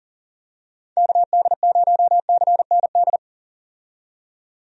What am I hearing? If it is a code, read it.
Morse code, heard as a keyed tone, about 40 wpm, 700 Hz. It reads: KD0CND
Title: USA Call Signs - 6 Characters - 40wpm